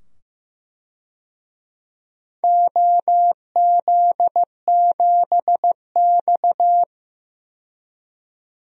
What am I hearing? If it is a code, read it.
Morse code, heard as a keyed tone, about 16 wpm, 700 Hz. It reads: OZ7X
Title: DXCC-100 Call Signs - 4 Characters - 15wpm